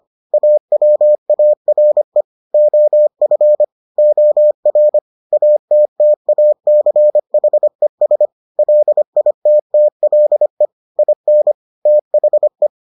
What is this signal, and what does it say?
Morse code, heard as a keyed tone, about 25 wpm, 600 Hz. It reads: AWARE OF OR ATTACHES LITTLE IN THE